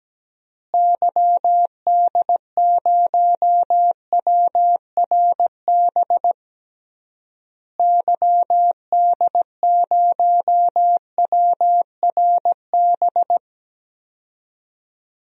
Morse code, heard as a keyed tone, about 17 wpm, 700 Hz. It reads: YD0WRB YD0WRB
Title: DXCC-100 Call Signs - 6 Characters - Repeated 3x - 17wpm